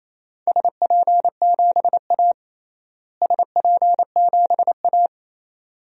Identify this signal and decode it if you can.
Morse code, heard as a keyed tone, about 27 wpm, 700 Hz. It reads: SP7A SP7A
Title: DXCC-100 Call Signs - 4 Characters - Repeated 2x - 28wpm